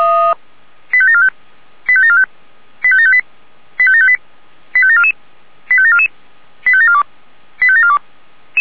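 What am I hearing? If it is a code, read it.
Five-tone selective call (ZVEI), heard as quick runs of five tones, tones of about 70 ms. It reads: 85745 85745 85758 85758 85744 85744 85742 85742
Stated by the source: ZVEI five-tone call received by SDR.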